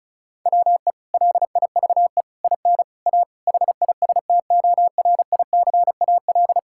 Morse code, heard as a keyed tone, about 36 wpm, 700 Hz. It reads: WE LIVE IN A HISTORICAL
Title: Sentences from Top 1000 Words - Review Part 5 - 35wpm